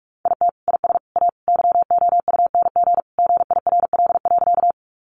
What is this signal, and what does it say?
Morse code, heard as a keyed tone, about 44 wpm, 700 Hz. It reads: IT IS A YOUNG GIRL.
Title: Sentences from Top 400 Words 45wpm